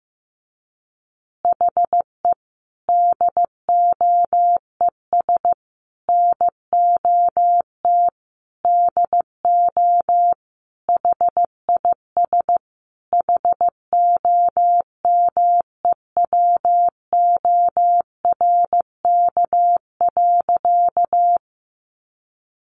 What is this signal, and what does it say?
Morse code, heard as a keyed tone, about 15 wpm, 700 Hz. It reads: HE DOES NOT DO HIS HOMEWORK.
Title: Additional Easy Phrases 15wpm